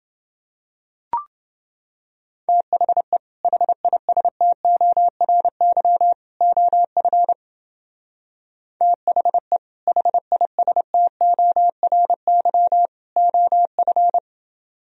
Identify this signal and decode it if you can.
Morse code, heard as a keyed tone, about 28 wpm, 700 Hz. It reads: THE HISTORY OF THE HISTORY OF